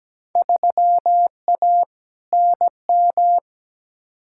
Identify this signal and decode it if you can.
Morse code, heard as a keyed tone, about 17 wpm, 700 Hz. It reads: 3A NM
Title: Field Day Exchanges Encore - 17wpm